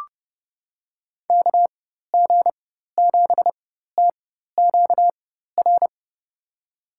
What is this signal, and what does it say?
Morse code, heard as a keyed tone, about 31 wpm, spaced as for 14 wpm, 700 Hz. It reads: KG7TQR
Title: USA Call Signs - 6 Characters - 4x Character Spacing - 30wpm